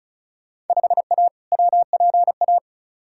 Morse code, heard as a keyed tone, about 36 wpm, 700 Hz. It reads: 5A WPA
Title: Field Day Exchanges Encore - 35wpm